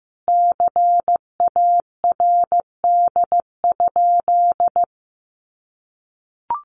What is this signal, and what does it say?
Morse code, heard as a keyed tone, about 15 wpm, 700 Hz. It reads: CARD?
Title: Sentences from Top 1100 Words 15wpm